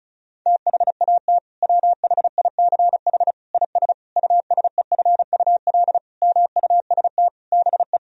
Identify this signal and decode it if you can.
Morse code, heard as a keyed tone, about 34 wpm, 700 Hz. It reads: THAT WHICH IS USEFUL MUST BE
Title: Sentences from Top 1000 Words - Review Part 4 - 35wpm